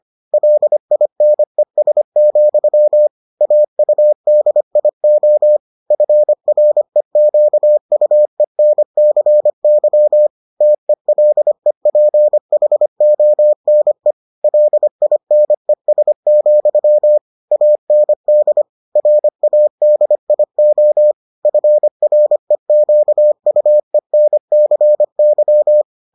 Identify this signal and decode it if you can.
Morse code, heard as a keyed tone, about 25 wpm, 600 Hz. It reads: LINES, AUDIO FREQUENCY TELEPHONE LINES, AND RADIO FREQUENCY